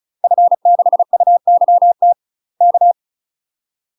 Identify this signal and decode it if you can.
Morse code, heard as a keyed tone, about 34 wpm, 700 Hz. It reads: F6UYT K
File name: UK MASTER QSOs TRAINING / 35WPM / UK MASTER QSOs 0030